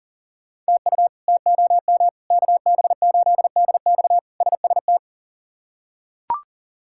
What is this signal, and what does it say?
Morse code, heard as a keyed tone, about 40 wpm, 700 Hz. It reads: TU TOM KB8DX SST
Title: K1USN SST QSO Practice - 40wpm